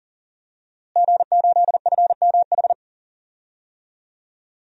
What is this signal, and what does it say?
Morse code, heard as a keyed tone, about 37 wpm, 700 Hz. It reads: G8FMH
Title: DXCC-100 Call Signs - 5 Characters - Repeated 3x - 40wpm